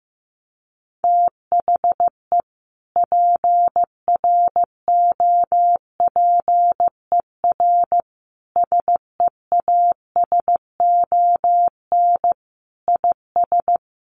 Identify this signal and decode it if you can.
Morse code, heard as a keyed tone, about 15 wpm, 700 Hz. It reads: THE PROPER SEASON IS
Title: Sentences from Top 1400 Words 15wpm